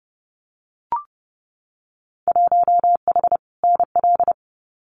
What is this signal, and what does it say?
Morse code, heard as a keyed tone, about 30 wpm, 700 Hz. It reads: 1H NL
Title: Winter Field Day Exchanges 30wpm